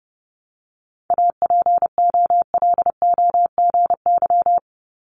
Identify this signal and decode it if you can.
Morse code, heard as a keyed tone, about 32 wpm, 700 Hz. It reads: APOLOGY
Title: Multi-letter Phonemes 30wpm